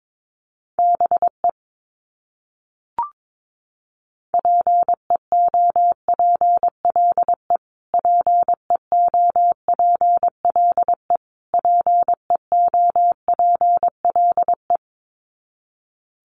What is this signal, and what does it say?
Morse code, heard as a keyed tone, about 22 wpm, 700 Hz. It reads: BE PEOPLE PEOPLE PEOPLE
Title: Top 300 Words Repeated 22wpm